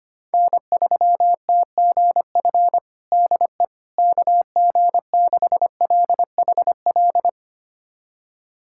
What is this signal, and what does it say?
Morse code, heard as a keyed tone, about 25 wpm, 700 Hz. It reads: N3TGF DE KG6LHL
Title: USA Callsign DE Callsign 25wpm